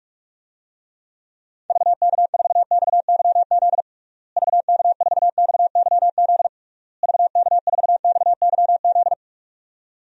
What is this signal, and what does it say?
Morse code, heard as a keyed tone, about 41 wpm, 700 Hz. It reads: VK4XYZ VK4XYZ VK4XYZ
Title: DXCC-100 Call Signs - 6 Characters - Repeated 3x - 45wpm